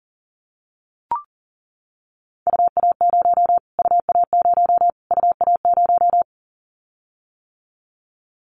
Morse code, heard as a keyed tone, about 40 wpm, 700 Hz. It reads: UA0 UA0 UA0
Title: Callsign Prefixes for Top 25 DXCC - Repeated 3x - 40wpm